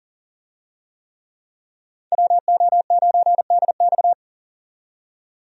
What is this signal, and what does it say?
Morse code, heard as a keyed tone, about 40 wpm, 700 Hz. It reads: WO9DX
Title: USA Call Signs - 5 Characters - 40wpm